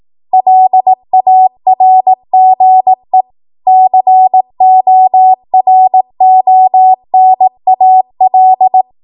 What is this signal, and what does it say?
Morse code, heard as a keyed tone, about 18 wpm, 750 Hz. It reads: LARGE CORONAL
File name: Morse / 18WPM